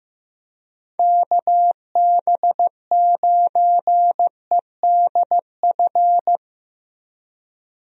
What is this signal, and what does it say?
Morse code, heard as a keyed tone, about 15 wpm, 700 Hz. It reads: KB9EDF
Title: USA Call Signs - 6 Characters - 15wpm